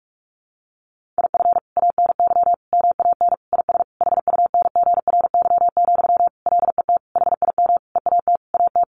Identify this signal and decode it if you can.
Morse code, heard as a keyed tone, about 44 wpm, 700 Hz. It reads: IF ANY MAN IS HUNGRY, LET HIM EAT AT
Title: Sentences from Top 1400 Words 45wpm